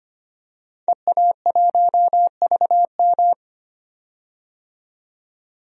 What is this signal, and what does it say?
Morse code, heard as a keyed tone, about 25 wpm, 700 Hz. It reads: EA1VM
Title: DXCC-100 Call Signs - 5 Characters - 25wpm